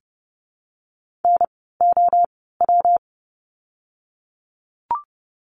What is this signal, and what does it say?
Morse code, heard as a keyed tone, about 30 wpm, spaced as for 17 wpm, 700 Hz. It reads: NOW
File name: course / 30:10wpm / 115 Lesson 19f - Letter w - Words 30wpm